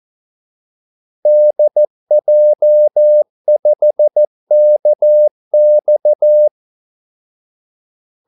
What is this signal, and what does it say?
Morse code, heard as a keyed tone, about 14 wpm, 600 Hz. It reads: DJ5KX
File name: Callsigns-14wpm